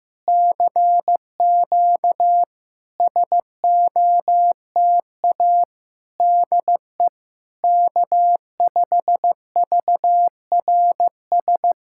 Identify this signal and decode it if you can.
Morse code, heard as a keyed tone, about 16 wpm, 700 Hz. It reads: CQ SOTA DE K5VRS
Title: SOTA QSO Practice 15wpm